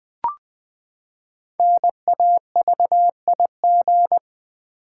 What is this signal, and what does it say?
Morse code, heard as a keyed tone, about 20 wpm, 700 Hz. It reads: NAVIG
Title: Build A Word - 5 Letters At A Time - 10-Letter Words - 20wpm